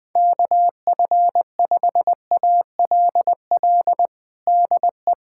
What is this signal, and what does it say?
Morse code, heard as a keyed tone, about 21 wpm, 700 Hz. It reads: KF5ALL DE
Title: USA Callsign DE Callsign - 6 Characters - 20wpm